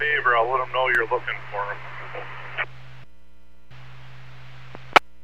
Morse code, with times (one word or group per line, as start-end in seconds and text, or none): none